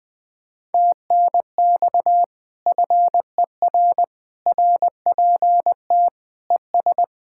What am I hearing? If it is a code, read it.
Morse code, heard as a keyed tone, about 20 wpm, 700 Hz. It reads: TNX FER RPT ES
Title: Speed-Racing Common QSO Phrases - 17wpm